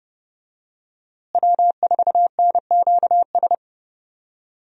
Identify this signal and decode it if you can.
Morse code, heard as a keyed tone, about 31 wpm, 700 Hz. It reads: W4NQS
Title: USA Call Signs - 5 Characters - Repeated 2x - 30wpm